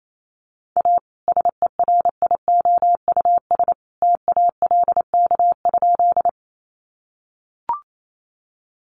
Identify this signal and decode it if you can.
Morse code, heard as a keyed tone, about 28 wpm, 700 Hz. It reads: A SERIOUS TALK?
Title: Sentences from Top 1100 Words 28wpm